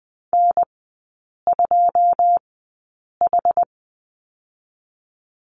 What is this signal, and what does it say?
Morse code, heard as a keyed tone, about 19 wpm, spaced as for 8 wpm, 690 Hz. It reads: N2H
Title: 081 Lesson 14e - Number 2 - Three Characters 20wpm